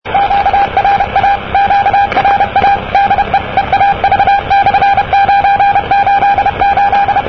Morse code, heard as a keyed tone, about 31 wpm, 800 Hz, with radio noise. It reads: QRA QRA DE EAV/988
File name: europe / e-539-cw-eav988-rb-1973